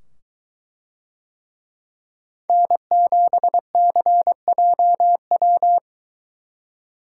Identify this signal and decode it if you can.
Morse code, heard as a keyed tone, about 24 wpm, 700 Hz. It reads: N7CJW